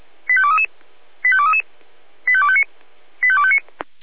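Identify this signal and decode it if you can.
Five-tone selective call (ZVEI), heard as quick runs of five tones, tones of about 70 ms. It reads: 85233 85233 85259 85259
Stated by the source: ZVEI five-tone selective call received on an SDR.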